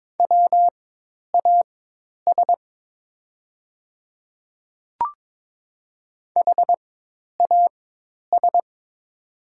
Morse code, heard as a keyed tone, about 22 wpm, spaced as for 10 wpm, 700 Hz. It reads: WAS HAS
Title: 4x Character Spacing - Top 100 Words - 22wpm